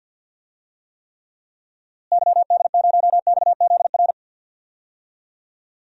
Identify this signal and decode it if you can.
Morse code, heard as a keyed tone, about 52 wpm, 700 Hz. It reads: YD0XZR
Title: DXCC-100 Call Signs - 6 Characters - Repeated 2x - 50wpm